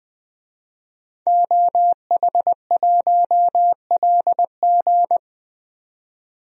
Morse code, heard as a keyed tone, about 20 wpm, 700 Hz. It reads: OH1LG
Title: DXCC-100 Call Signs - 5 Characters - 20wpm